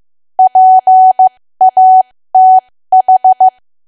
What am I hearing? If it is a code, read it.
Morse code, heard as a keyed tone, about 15 wpm, spaced as for 13 wpm, 750 Hz. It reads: PATH
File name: cwa / W1AW-Level2 / 13WPM